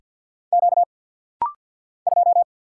Morse code, heard as a keyed tone, about 48 wpm, 700 Hz. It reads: Q 2
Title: Single Letter-Number - Rapid-Fire 50wpm